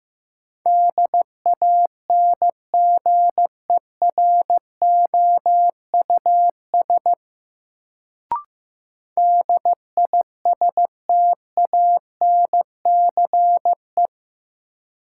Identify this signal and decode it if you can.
Morse code, heard as a keyed tone, about 15 wpm, 700 Hz. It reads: DANGEROUS DISTANCE
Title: Sentences from Top 1200 Words - New Word Drill - 15wpm